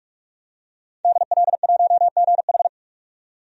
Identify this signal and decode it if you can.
Morse code, heard as a keyed tone, about 45 wpm, 700 Hz. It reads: DL1GH
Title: DXCC-100 Call Signs - 5 Characters - 45wpm